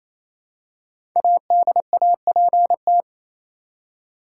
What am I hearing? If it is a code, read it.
Morse code, heard as a keyed tone, about 28 wpm, 700 Hz. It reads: ADAPT
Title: Five Letter Words - 28wpm